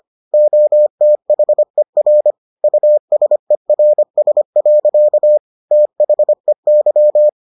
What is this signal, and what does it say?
Morse code, heard as a keyed tone, about 25 wpm, 600 Hz. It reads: OTHER USERS. THEY